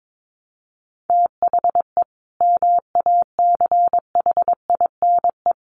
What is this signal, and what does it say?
Morse code, heard as a keyed tone, about 22 wpm, 700 Hz. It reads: THE MACHINE
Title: A Sound Of Thunder 22wpm